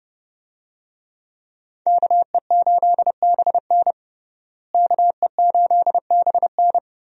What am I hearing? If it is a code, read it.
Morse code, heard as a keyed tone, about 30 wpm, 700 Hz. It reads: KE8BN KE8BN
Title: USA Call Signs - 5 Characters - Repeated 3x - 30wpm